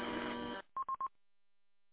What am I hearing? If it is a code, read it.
There is no code.